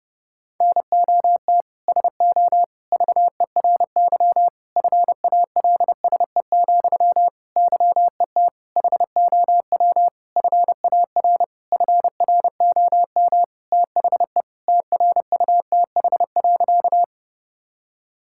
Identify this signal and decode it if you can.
Morse code, heard as a keyed tone, about 32 wpm, 700 Hz. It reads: NOT SO VERY FALSE, YET HOW FAR FROM THE TRUTH.
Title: Sentences from Top 1600 Words 30wpm